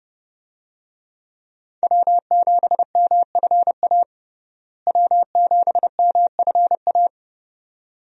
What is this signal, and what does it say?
Morse code, heard as a keyed tone, about 30 wpm, 700 Hz. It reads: W7MFA W7MFA
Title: USA Call Signs - 5 Characters - Repeated 2x - 30wpm